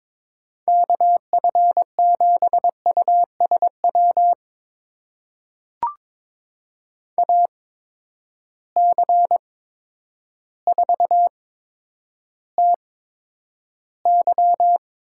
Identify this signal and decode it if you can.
Morse code, heard as a keyed tone, about 22 wpm, spaced as for 10 wpm, 700 Hz. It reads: KF7USW AC4TY
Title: USA Call Signs - 8x Character Spacing - 22wpm